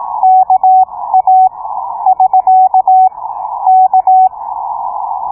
Morse code, heard as a keyed tone, about 17 wpm, spaced as for 15 wpm, 750 Hz. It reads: KA <SK> K